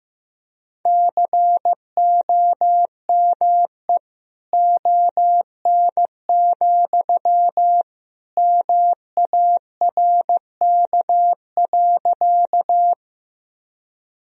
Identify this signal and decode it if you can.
Morse code, heard as a keyed tone, about 15 wpm, 700 Hz. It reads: COME ON, MARK.
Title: Sentences from Top 1000 Words - Review Part 1 - 15wpm